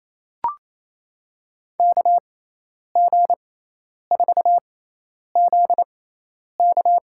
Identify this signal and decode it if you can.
Morse code, heard as a keyed tone, about 27 wpm, spaced as for 10 wpm, 700 Hz. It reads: KG4ZK
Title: USA Call Signs - 6 Characters - 6x Character Spacing - 28wpm